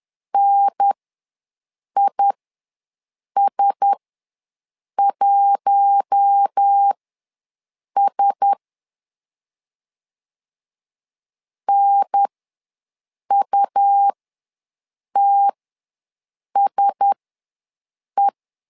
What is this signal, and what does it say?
Morse code, heard as a keyed tone, about 10 wpm, spaced as for 6 wpm, 800 Hz. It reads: NIS1S NUTSE
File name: LEZIONI / 4 Q1S ws 10wpm 2208